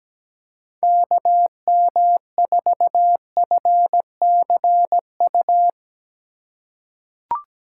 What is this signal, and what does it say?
Morse code, heard as a keyed tone, about 17 wpm, 700 Hz. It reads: KM4FCU
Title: USA Call Signs - 6 Characters - 4x Character Spacing - 17wpm